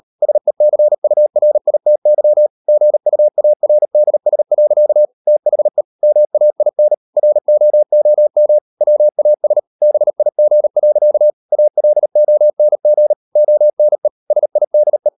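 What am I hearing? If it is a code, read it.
Morse code, heard as a keyed tone, about 38 wpm, 600 Hz. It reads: SECURITY GUARDS. THE MAIN ROOM WAS BIG. ALONG ONE SIDE
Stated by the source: Morse code practice at 38 WPM, sending Ken Follett's novel Never in English, from CW med Gnister.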